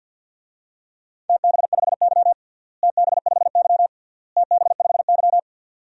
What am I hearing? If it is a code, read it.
Morse code, heard as a keyed tone, about 46 wpm, 700 Hz. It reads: TB5Y TB5Y TB5Y
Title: DXCC-100 Call Signs - 4 Characters - Repeated 3x - 50wpm